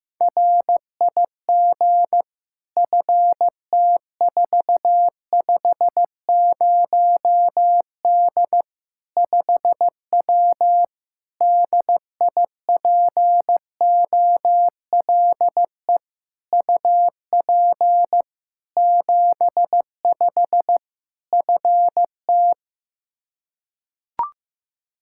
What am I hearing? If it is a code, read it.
Morse code, heard as a keyed tone, about 16 wpm, 700 Hz. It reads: RIG FT450D 5W DIPOLE UP 75 FT
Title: Practice QSO - Exchange 2 - Info - 15wpm